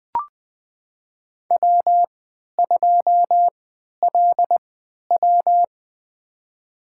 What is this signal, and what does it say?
Morse code, heard as a keyed tone, about 20 wpm, spaced as for 11 wpm, 700 Hz. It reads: W2LW
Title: USA Call Signs - 4 Characters - 3x Character Spacing - 20wpm